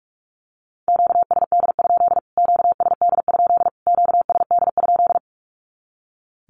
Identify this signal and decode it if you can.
Morse code, heard as a keyed tone, about 44 wpm, 700 Hz. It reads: QSD? QSD? QSD?
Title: Q-Codes Repeated 3x 45wpm